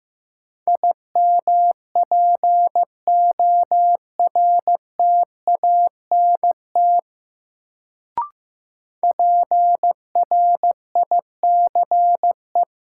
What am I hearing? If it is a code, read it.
Morse code, heard as a keyed tone, about 15 wpm, 700 Hz. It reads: IMPORTANT PRICE
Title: Sentences from Top 600 Words - New Word Drill - 15wpm